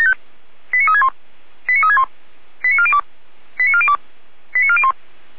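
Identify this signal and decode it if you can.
Five-tone selective call (ZVEI), heard as quick runs of five tones, tones of about 70 ms. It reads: 89361 89361 89402 89402 89401
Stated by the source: Five-tone ZVEI selective call received on an SDR.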